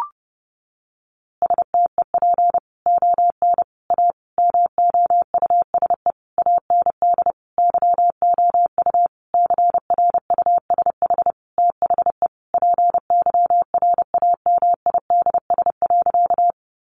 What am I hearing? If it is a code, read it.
Morse code, heard as a keyed tone, about 30 wpm, 700 Hz. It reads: STEP ON A MOUSE AND YOU CRUSH THE PYRAMIDS.
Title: A Sound Of Thunder 30wpm